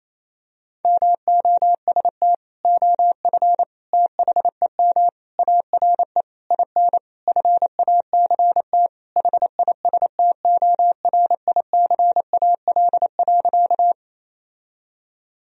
Morse code, heard as a keyed tone, about 28 wpm, 700 Hz. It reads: MOST OF THEM ARE IN FACT HISTORICAL.
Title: Sentences from Top 1000 Words - Review Part 4 - 28wpm